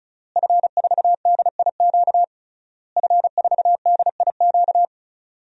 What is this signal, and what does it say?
Morse code, heard as a keyed tone, about 36 wpm, 700 Hz. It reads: F4DIQ F4DIQ
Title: DXCC-100 Call Signs - 5 Characters - Repeated 3x - 35wpm